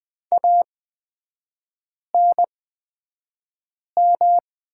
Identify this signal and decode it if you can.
Morse code, heard as a keyed tone, about 20 wpm, spaced as for 5 wpm, 700 Hz. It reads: ANM
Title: 243 Lesson 39e - Letter z - Three Characters 20wpm